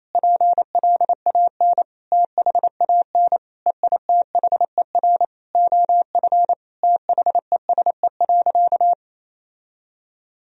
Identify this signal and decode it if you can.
Morse code, heard as a keyed tone, about 28 wpm, 700 Hz. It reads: PLAN THAN EITHER OF THESE.
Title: Sentences from Top 1000 Words - Review Part 4 - 28wpm